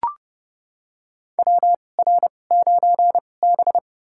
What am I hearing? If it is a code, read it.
Morse code, heard as a keyed tone, about 30 wpm, spaced as for 22 wpm, 700 Hz. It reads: WR9B
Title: USA Call Signs - 4 Characters - 2x Character Spacing - 30wpm